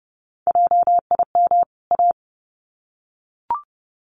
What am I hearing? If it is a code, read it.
Morse code, heard as a keyed tone, about 30 wpm, 700 Hz. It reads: JIM A
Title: CWT Exchange 30wpm